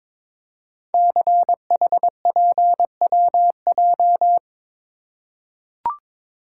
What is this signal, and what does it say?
Morse code, heard as a keyed tone, about 22 wpm, 700 Hz. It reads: CHPWJ
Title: Code Groups - 5 Letters - 22wpm